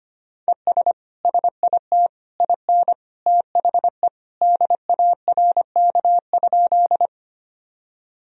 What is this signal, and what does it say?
Morse code, heard as a keyed tone, about 25 wpm, 700 Hz. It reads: ES SIT IN THE DARK?
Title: Sentences from Top 500 Words and Abbreviations 25wpm